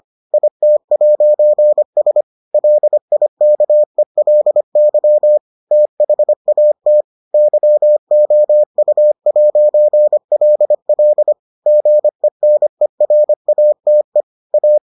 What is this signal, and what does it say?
Morse code, heard as a keyed tone, about 25 wpm, 600 Hz. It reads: IT'S LIKELY THAT YOU'LL GENERATE A